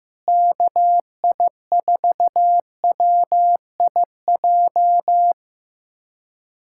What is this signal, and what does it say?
Morse code, heard as a keyed tone, about 15 wpm, 700 Hz. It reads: KI4WIJ
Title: USA Callsign DE Callsign - 6 Characters - 15wpm